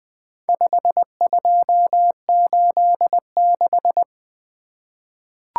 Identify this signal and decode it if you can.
Morse code, heard as a keyed tone, about 20 wpm, 700 Hz. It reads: 5286
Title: Four Letter-Number Combinations - 20wpm